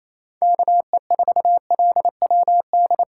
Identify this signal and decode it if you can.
Morse code, heard as a keyed tone, about 28 wpm, 700 Hz. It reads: KE4LWD
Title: USA Call Signs - 8x Character Spacing - 28wpm